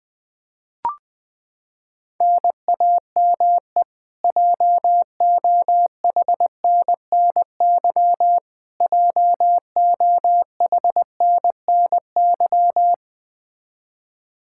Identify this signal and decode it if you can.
Morse code, heard as a keyed tone, about 21 wpm, 700 Hz. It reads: NAME JOHNNY JOHNNY
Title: Practice QSO - Exchange 1 - Beginning - 20wpm